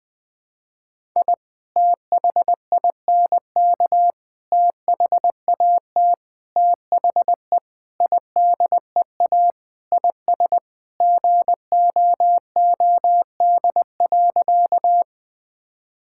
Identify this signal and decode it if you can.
Morse code, heard as a keyed tone, about 20 wpm, 700 Hz. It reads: I THINK THAT THE IDEA IS GOOD.
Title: Sentences with 4-Word Phrases - 20wpm